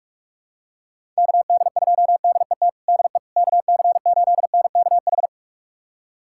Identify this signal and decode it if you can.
Morse code, heard as a keyed tone, about 43 wpm, 700 Hz. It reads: KD2DET DE KC8NKH
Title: USA Callsign DE Callsign - 6 Characters - 45wpm